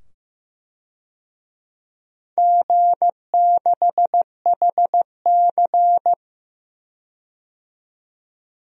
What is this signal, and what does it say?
Morse code, heard as a keyed tone, about 15 wpm, 700 Hz. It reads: G6HC
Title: DXCC-100 Call Signs - 4 Characters - 15wpm